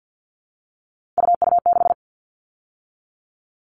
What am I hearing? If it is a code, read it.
Morse code, heard as a keyed tone, about 50 wpm, 700 Hz. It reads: UU6
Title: All Cut Numbers - 3-Digit 50wpm